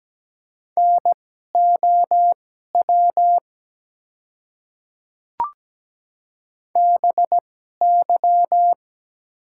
Morse code, heard as a keyed tone, about 17 wpm, spaced as for 12 wpm, 700 Hz. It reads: NOW BY
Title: 2x Character Spacing - Top 100 Words - 17wpm